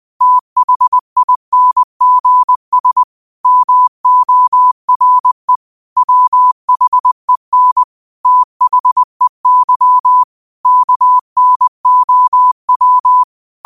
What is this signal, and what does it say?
Morse code, heard as a keyed tone, about 20 wpm, 1000 Hz. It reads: THINGS MORE WHEN THEY KNOW